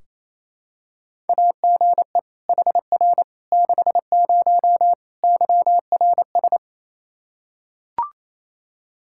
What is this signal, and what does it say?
Morse code, heard as a keyed tone, about 28 wpm, 700 Hz. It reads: AGE HR 60 YRS
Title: Practice QSO - Exchange 3 - More Info - 28wpm